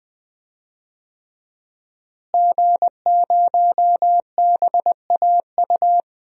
Morse code, heard as a keyed tone, about 21 wpm, 700 Hz. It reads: G0BAU
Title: DXCC-100 Call Signs - 5 Characters - Repeated 3x - 20wpm